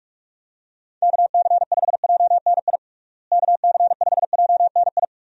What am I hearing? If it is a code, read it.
Morse code, heard as a keyed tone, about 46 wpm, 700 Hz. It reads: KC5JNI KC5JNI
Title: USA Call Signs - 6 Characters - Repeated 3x - 45wpm